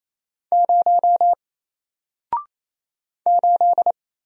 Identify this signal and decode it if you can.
Morse code, heard as a keyed tone, about 28 wpm, 700 Hz. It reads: 0 8
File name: Single-Letter-Number-28wpm